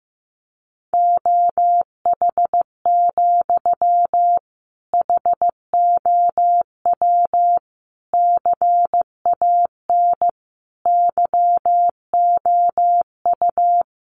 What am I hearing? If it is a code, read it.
Morse code, heard as a keyed tone, about 15 wpm, 700 Hz. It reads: OH, HOW CAN YOU
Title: Sentences from Top 500 Words 15wpm